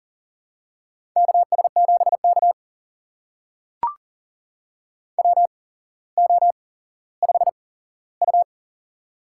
Morse code, heard as a keyed tone, about 40 wpm, spaced as for 17 wpm, 700 Hz. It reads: KS7K WO5U
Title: USA Call Signs - 4 Characters - 8x Character Spacing - 40wpm